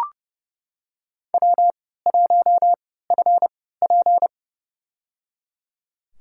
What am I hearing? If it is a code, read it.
Morse code, heard as a keyed tone, about 30 wpm, spaced as for 17 wpm, 700 Hz. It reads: W1FP
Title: USA Call Signs - 4 Characters - 3x Character Spacing - 30wpm